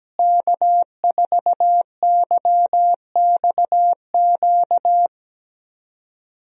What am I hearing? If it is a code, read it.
Morse code, heard as a keyed tone, about 17 wpm, 700 Hz. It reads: K4YXQ
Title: USA Call Signs - 5 Characters - Repeated 3x - 17wpm